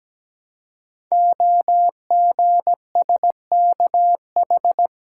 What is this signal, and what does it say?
Morse code, heard as a keyed tone, about 17 wpm, 700 Hz. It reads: OGSKH
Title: Code Groups - 5 Letters - 17wpm